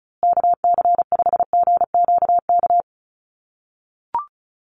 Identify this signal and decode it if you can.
Morse code, heard as a keyed tone, about 34 wpm, 700 Hz. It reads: KC5GQK
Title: USA Call Signs Repeated 2x 35wpm